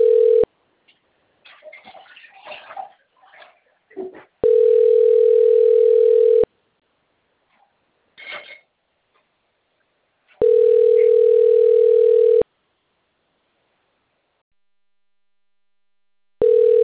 Ringback tone (North America), 440 + 480 Hz.